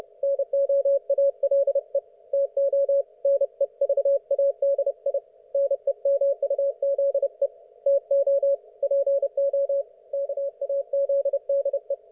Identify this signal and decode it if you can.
Morse code, heard as a keyed tone, about 30 wpm, 560 Hz. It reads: NO ALE TO NEVADI NEMUZE TO PO KAZDE
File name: Archiv / CUC179 / ok1fhi cul179